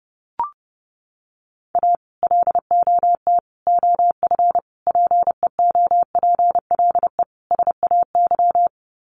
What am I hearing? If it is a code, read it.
Morse code, heard as a keyed tone, about 30 wpm, 700 Hz. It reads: A LOT OF PEOPLE SAY
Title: Sets of 5 Words Encore2 30wpm